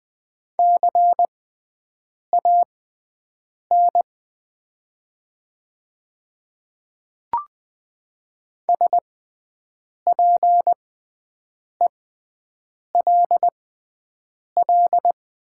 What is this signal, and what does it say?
Morse code, heard as a keyed tone, about 20 wpm, spaced as for 7 wpm, 700 Hz. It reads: CAN SPELL
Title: 6x Character Spacing - Top 300 Words - 20wpm